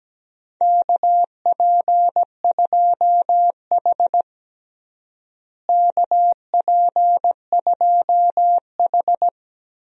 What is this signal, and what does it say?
Morse code, heard as a keyed tone, about 17 wpm, 700 Hz. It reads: KP2H KP2H
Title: DXCC-100 Call Signs - 4 Characters - Repeated 2x - 17wpm